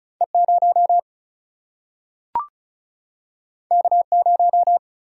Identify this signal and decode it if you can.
Morse code, heard as a keyed tone, about 36 wpm, 700 Hz. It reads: E 0 K 0
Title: Single-Letter Single-Number 35wpm